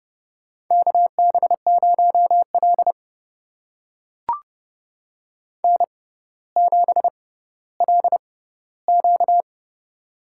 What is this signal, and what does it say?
Morse code, heard as a keyed tone, about 30 wpm, spaced as for 15 wpm, 700 Hz. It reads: KB0L N7LQ
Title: USA Call Signs - 4 Characters - 6x Character Spacing - 30wpm